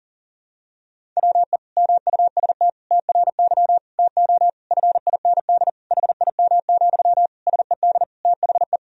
Morse code, heard as a keyed tone, about 40 wpm, 700 Hz. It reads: WE MUST TRY TO FIND HIM, SED THE
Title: Sentences from Top 400 Words and Abbreviations 40wpm